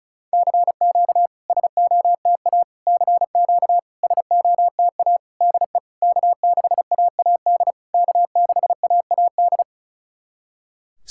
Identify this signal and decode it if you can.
Morse code, heard as a keyed tone, about 36 wpm, 700 Hz. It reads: CQ SOTA CQ SOTA DE K6AAD K6AAD
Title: SOTA QSO Practice 35wpm